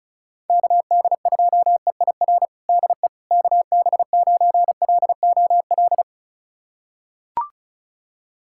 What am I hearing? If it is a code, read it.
Morse code, heard as a keyed tone, about 34 wpm, 700 Hz. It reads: KD2EIR DE KB9LOL
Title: USA Callsign DE Callsign - 6 Characters - 35wpm